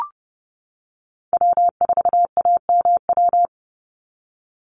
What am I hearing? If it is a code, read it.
Morse code, heard as a keyed tone, about 30 wpm, 700 Hz. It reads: W4AMW
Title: NA SOTA Chaser Callsigns 30wpm